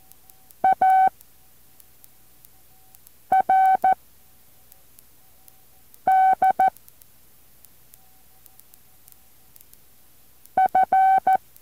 Morse code, heard as a keyed tone, about 14 wpm, 760 Hz. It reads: A R D F
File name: intl morse code2